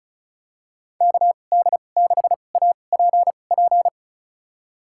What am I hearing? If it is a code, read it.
Morse code, heard as a keyed tone, about 36 wpm, spaced as for 25 wpm, 690 Hz. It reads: KD6APP
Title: USA Call Signs - 6 Characters - 2x Character Spacing - 35wpm